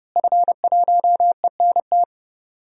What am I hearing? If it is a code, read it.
Morse code, heard as a keyed tone, about 30 wpm, 700 Hz. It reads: F1ENT
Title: DXCC-100 Call Signs - 5 Characters - Repeated 2x - 30wpm